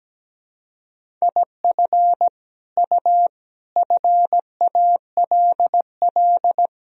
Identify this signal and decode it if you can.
Morse code, heard as a keyed tone, about 17 wpm, 700 Hz. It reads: IF U FALL
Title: Sentences from Top 500 Words and Abbreviations 17wpm